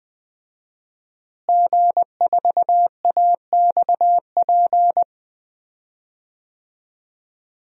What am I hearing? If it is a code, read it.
Morse code, heard as a keyed tone, about 20 wpm, 700 Hz. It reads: G4AXP
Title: DXCC-100 Call Signs - 5 Characters - Repeated 2x - 20wpm